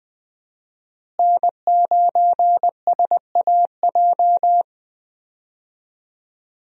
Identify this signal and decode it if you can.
Morse code, heard as a keyed tone, about 20 wpm, 700 Hz. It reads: N9SAJ
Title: USA Call Signs - 5 Characters - Repeated 3x - 20wpm